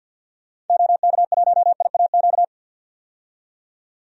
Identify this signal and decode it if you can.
Morse code, heard as a keyed tone, about 51 wpm, 700 Hz. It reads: OK1IAQ
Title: DXCC-100 Call Signs - 6 Characters - 50wpm